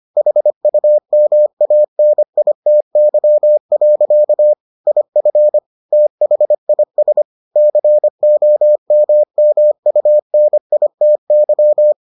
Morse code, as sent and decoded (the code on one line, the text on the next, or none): .... ..- -- .- -. .. - -.-- .-.-.- / .. ..-. / - .... .. ... / -.-. --- -- -- ..- -. .. - -.--
HUMANITY. IF THIS COMMUNITY